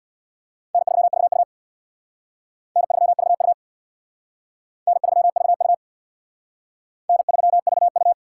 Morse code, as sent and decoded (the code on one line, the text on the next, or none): -. ...-- ...- ..- / -. ...-- ...- ..- / -. ...-- ...- ..- / -. ...-- ...- ..-
N3VU N3VU N3VU N3VU